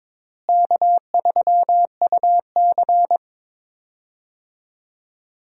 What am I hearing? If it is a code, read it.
Morse code, heard as a keyed tone, about 22 wpm, 700 Hz. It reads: K3UC
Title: USA Call Signs - 4 Characters - Repeated 2x - 22wpm